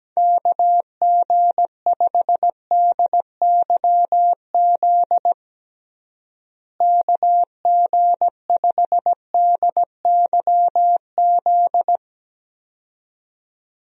Morse code, heard as a keyed tone, about 17 wpm, 700 Hz. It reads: KG5DYZ KG5DYZ
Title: USA Call Signs - 6 Characters - Repeated 2x - 17wpm